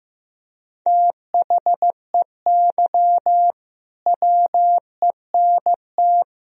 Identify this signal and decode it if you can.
Morse code, heard as a keyed tone, about 15 wpm, 700 Hz. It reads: THEY WENT
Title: Sentences from Top 1000 Words - Review Part 1 - 15wpm